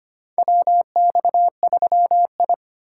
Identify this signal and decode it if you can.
Morse code, heard as a keyed tone, about 25 wpm, 700 Hz. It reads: WX3I
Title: USA Call Signs - 3x Character Spacing - 25wpm